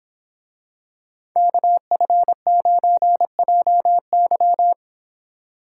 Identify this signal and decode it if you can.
Morse code, heard as a keyed tone, about 26 wpm, 700 Hz. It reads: KF9JY